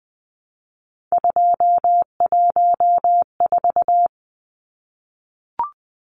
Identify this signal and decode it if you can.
Morse code, heard as a keyed tone, about 20 wpm, 700 Hz. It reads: 214
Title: Numbers - 3-Digit 20wpm